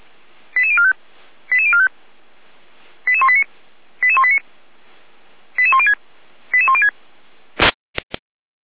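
Five-tone selective call (ZVEI), heard as quick runs of five tones, tones of about 70 ms. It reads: 88045 88045 88189 88189 88106 88106
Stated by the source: Five-tone ZVEI selective call received by SDR.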